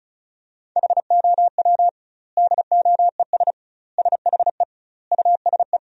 Morse code, heard as a keyed tone, about 36 wpm, 700 Hz. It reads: HOW DOES SHE USE
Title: 2x Word-Spacing - Sentences from Top 200 Words - 35wpm